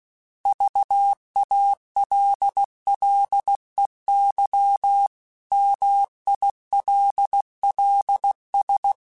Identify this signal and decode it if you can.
Morse code, heard as a keyed tone, about 16 wpm, 800 Hz. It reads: VALLEY MILLS
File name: fromSMs / FromSMs mp3s / Cities@15